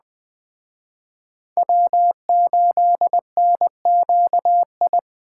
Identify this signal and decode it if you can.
Morse code, heard as a keyed tone, about 21 wpm, 700 Hz. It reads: W8NQI